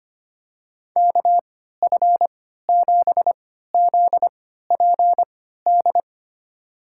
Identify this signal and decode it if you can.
Morse code, heard as a keyed tone, about 25 wpm, spaced as for 14 wpm, 700 Hz. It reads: KF7ZPD
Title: USA Call Signs - 3x Character Spacing - 25wpm